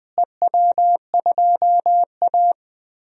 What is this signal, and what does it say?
Morse code, heard as a keyed tone, about 21 wpm, 700 Hz. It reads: EW2A
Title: DXCC-100 Call Signs - 4 Characters - Repeated 2x - 20wpm